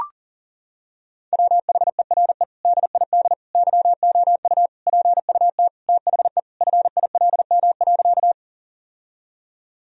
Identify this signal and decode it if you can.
Morse code, heard as a keyed tone, about 40 wpm, 700 Hz. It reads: WHERE DID YOU PUT THE FILM.
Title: Sentences from Top 1000 Words - Review Part 3 - 40wpm